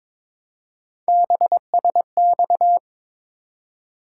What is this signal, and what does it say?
Morse code, heard as a keyed tone, about 22 wpm, 700 Hz. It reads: BSX
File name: practice / 22wpm / Three Letter-Number Combinations - 22wpm